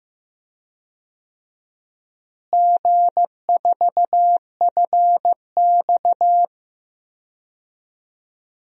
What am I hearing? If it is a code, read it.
Morse code, heard as a keyed tone, about 15 wpm, 700 Hz. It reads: G4FX
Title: DXCC-100 Call Signs - 4 Characters - Repeated 3x - 15wpm